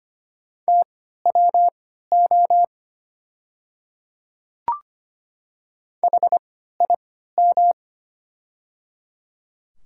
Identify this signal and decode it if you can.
Morse code, heard as a keyed tone, about 25 wpm, spaced as for 14 wpm, 700 Hz. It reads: TWO HIM
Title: 3x Character Spacing - Top 100 Words - 25wpm